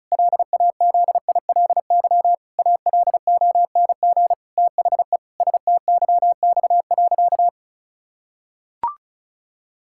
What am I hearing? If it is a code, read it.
Morse code, heard as a keyed tone, about 34 wpm, 700 Hz. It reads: LAZILY ALONG THE STYX.
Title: Pangram Challenge v2 - 35wpm